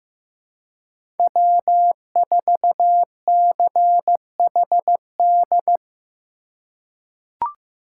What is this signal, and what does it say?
Morse code, heard as a keyed tone, about 15 wpm, 700 Hz. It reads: W4CHD
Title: USA Call Signs - 5 Characters - 2x Character Spacing - 15wpm